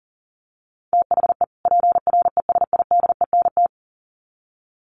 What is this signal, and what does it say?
Morse code, heard as a keyed tone, about 40 wpm, 700 Hz. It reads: THE PRESIDENT
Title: Sets of 2 Words 40wpm